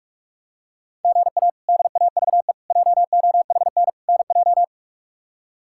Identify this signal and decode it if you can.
Morse code, heard as a keyed tone, about 42 wpm, 700 Hz. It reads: GA DAVE JOHN NJ
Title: K1USN SST QSO Practice - 45wpm